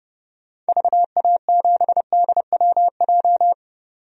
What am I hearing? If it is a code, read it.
Morse code, heard as a keyed tone, about 31 wpm, 700 Hz. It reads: VA7DWJ
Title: DXCC-100 Call Signs - 6 Characters - 30wpm